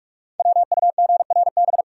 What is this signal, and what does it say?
Morse code, heard as a keyed tone, about 44 wpm, 700 Hz. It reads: WUGRB
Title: Code Groups - 5 Letters - 45wpm